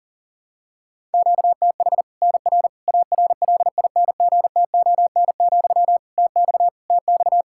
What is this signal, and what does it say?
Morse code, heard as a keyed tone, about 40 wpm, 700 Hz. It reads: QTH NR ARLINGTON, TX TX
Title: K1USN SST QSO Practice 40wpm